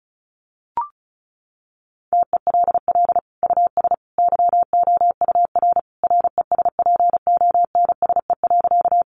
Morse code, heard as a keyed tone, about 34 wpm, 700 Hz. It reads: TELL US YOUR RESPONSE.
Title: Sentences from Top 1400 Words 35wpm